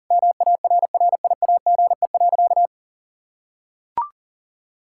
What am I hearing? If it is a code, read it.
Morse code, heard as a keyed tone, about 40 wpm, 700 Hz. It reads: MARRIAGE.
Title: Sentences from Top 1000 Words - Review Part 5 - 40wpm